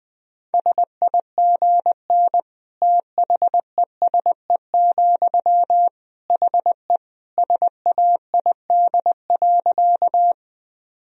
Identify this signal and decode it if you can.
Morse code, heard as a keyed tone, about 21 wpm, 700 Hz. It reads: SIGN THESE, HE SAID.
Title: Sentences from Top 1000 Words - Review Part 1 - 20wpm